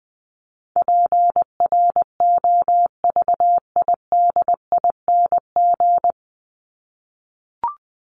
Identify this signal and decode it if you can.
Morse code, heard as a keyed tone, about 20 wpm, 700 Hz. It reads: PROVIDING
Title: ING Suffix - Top 100 Words with Base Words 20wpm